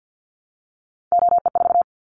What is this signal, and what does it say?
Morse code, heard as a keyed tone, about 50 wpm, 700 Hz. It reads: OE4
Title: Callsign Prefixes for Top 25 DXCC - Repeated 3x - 50wpm